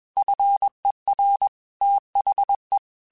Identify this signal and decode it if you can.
Morse code, heard as a keyed tone, about 21 wpm, 800 Hz. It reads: FER THE
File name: QSOs / QSOs-mp3s / kc7sac@20